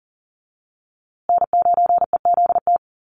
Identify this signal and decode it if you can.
Morse code, heard as a keyed tone, about 40 wpm, 700 Hz. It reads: N9EZT
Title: USA Call Signs Repeated 3x 40wpm